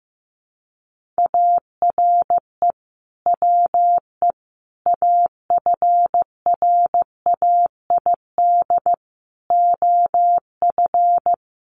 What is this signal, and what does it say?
Morse code, heard as a keyed tone, about 15 wpm, 700 Hz. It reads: ARE WE AFRAID OF
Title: Sentences from Top 1500 Words 15wpm